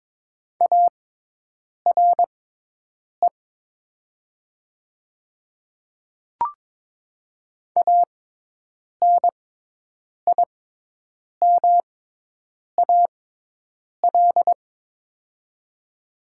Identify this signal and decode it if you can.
Morse code, heard as a keyed tone, about 22 wpm, spaced as for 8 wpm, 700 Hz. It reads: ARE ANIMAL
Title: 6x Character Spacing - Top 200 Words - 22wpm